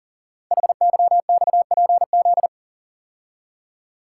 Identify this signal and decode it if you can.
Morse code, heard as a keyed tone, about 41 wpm, 690 Hz. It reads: HYXPZ
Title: Code Groups - 5 Letters - 40wpm